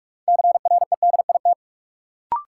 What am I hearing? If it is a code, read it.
Morse code, heard as a keyed tone, about 44 wpm, 700 Hz. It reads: CREDIT
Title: Sentences from Top 1000 Words - New Word Drill - 45wpm